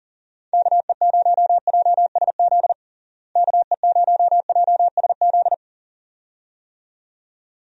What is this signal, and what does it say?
Morse code, heard as a keyed tone, about 40 wpm, 700 Hz. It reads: KE0JSZ KE0JSZ
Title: USA Call Signs - 6 Characters - Repeated 2x - 40wpm